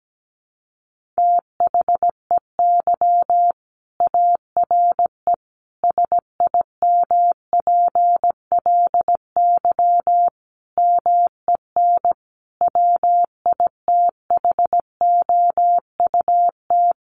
Morse code, heard as a keyed tone, about 17 wpm, 700 Hz. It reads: THEY ARE SIMPLY MEN WITHOUT
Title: Sentences from Top 1500 Words 17wpm